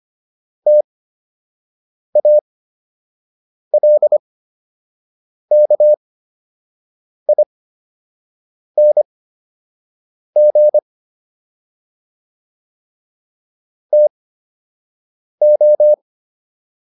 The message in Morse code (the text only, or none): TALKING TO